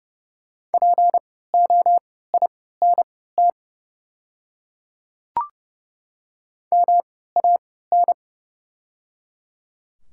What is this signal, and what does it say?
Morse code, heard as a keyed tone, about 29 wpm, spaced as for 17 wpm, 690 Hz. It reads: POINT MAN
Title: 3x Character Spacing - Top 200 Words - 30wpm